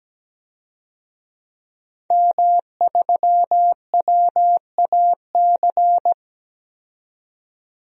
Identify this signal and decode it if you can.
Morse code, heard as a keyed tone, about 17 wpm, 700 Hz. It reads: M3WAC
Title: DXCC-100 Call Signs - 5 Characters - 17wpm